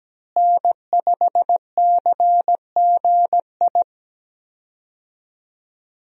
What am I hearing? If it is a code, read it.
Morse code, heard as a keyed tone, about 17 wpm, 700 Hz. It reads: N5CGI
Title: USA Call Signs - 5 Characters - Repeated 2x - 17wpm